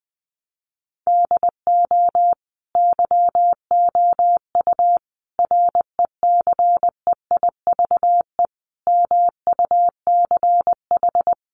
Morse code, heard as a keyed tone, about 20 wpm, 700 Hz. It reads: DO YOU RECEIVE MUCH
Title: Sentences from Top 1000 Words 20wpm